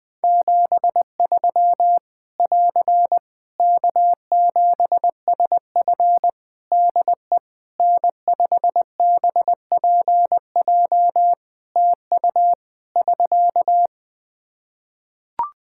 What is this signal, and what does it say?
Morse code, heard as a keyed tone, about 20 wpm, 700 Hz. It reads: 73 <AR> K7SF DE N5BPJ TU <SK>
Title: Practice QSO - Exchange 4 - Ending - 20wpm